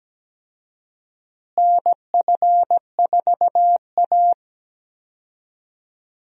Morse code, heard as a keyed tone, about 17 wpm, 700 Hz. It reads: NF4A
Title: USA Call Signs - 4 Characters - Repeated 2x - 17wpm